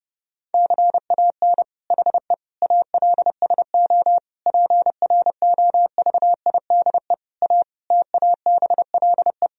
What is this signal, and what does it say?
Morse code, heard as a keyed tone, about 32 wpm, 700 Hz. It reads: CAN HE ALSO PROVIDE A TABLE
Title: Sentences from Top 1000 Words - Review Part 3 - 30wpm